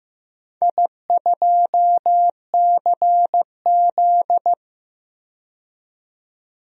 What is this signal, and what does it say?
Morse code, heard as a keyed tone, about 15 wpm, 700 Hz. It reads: I2CZ
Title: DXCC-100 Call Signs - 4 Characters - 15wpm